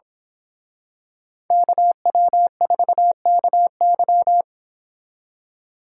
Morse code, heard as a keyed tone, about 26 wpm, 700 Hz. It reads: KW4KY
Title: Speed Racing - USA Call Signs 22wpm